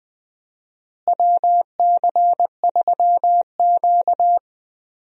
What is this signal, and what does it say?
Morse code, heard as a keyed tone, about 20 wpm, 700 Hz. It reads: WC3Q
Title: USA Call Signs - 4 Characters - Repeated 2x - 20wpm